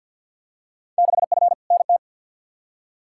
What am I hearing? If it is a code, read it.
Morse code, heard as a keyed tone, about 51 wpm, 700 Hz. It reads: 6F NT
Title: Field Day Exchanges Encore - 50wpm